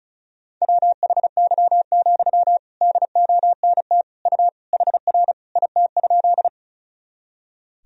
Morse code, heard as a keyed tone, about 36 wpm, 700 Hz. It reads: WHY, DONT U HR IT?
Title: Sentences from Top 300 Words and Abbreviations 35wpm